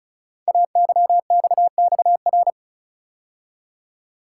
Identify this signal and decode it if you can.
Morse code, heard as a keyed tone, about 36 wpm, 690 Hz. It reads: AYXXR
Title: Code Groups - 5 Letters - 35wpm